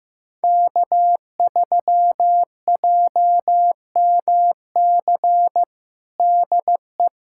Morse code, heard as a keyed tone, about 15 wpm, 700 Hz. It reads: K3JMC DE
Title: USA Callsign DE Callsign - 5 Characters - 15wpm